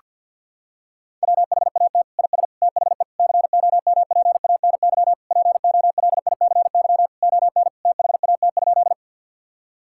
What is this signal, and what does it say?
Morse code, heard as a keyed tone, about 52 wpm, 700 Hz. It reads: WHAT IS THE COMPANY POLICY ON THAT?
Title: Sentences from Top 1000 Words - Review Part 5 - 50wpm